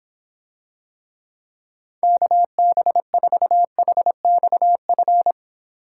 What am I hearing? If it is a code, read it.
Morse code, heard as a keyed tone, about 24 wpm, 700 Hz. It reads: KB4HXF